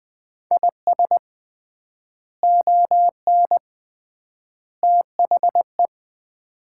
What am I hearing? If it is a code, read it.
Morse code, heard as a keyed tone, about 20 wpm, 700 Hz. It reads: IS ON THE
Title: 3x Word-Spacing - Sets of 3 Words - 20wpm